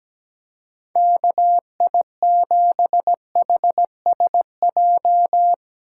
Morse code, heard as a keyed tone, about 17 wpm, 700 Hz. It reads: KI7HSJ
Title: USA Call Signs - 6 Characters - 4x Character Spacing - 17wpm